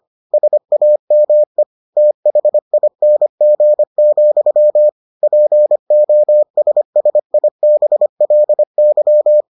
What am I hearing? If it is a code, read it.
Morse code, heard as a keyed tone, about 25 wpm, 600 Hz. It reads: SAME THING, POSSIBLY